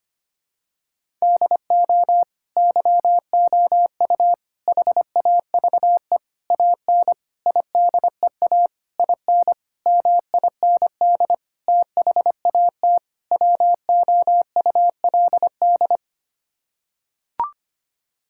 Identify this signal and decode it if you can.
Morse code, heard as a keyed tone, about 25 wpm, 700 Hz. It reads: DO YOU HAVE AN IDEA IN MIND THAT WOULD
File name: Sets of 9 Words - 25wpm